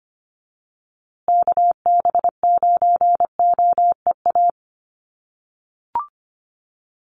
Morse code, heard as a keyed tone, about 25 wpm, 700 Hz. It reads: KB9OEA
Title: USA Call Signs Repeated 2x 25wpm